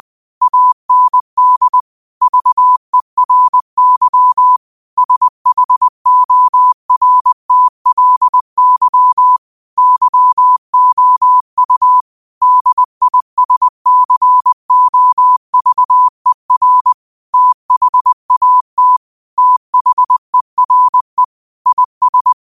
Morse code, heard as a keyed tone, about 20 wpm, 1000 Hz. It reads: AND VERY SHORTLY YOU DISCOVER THAT THERE IS
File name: qotdfiles / 20wpm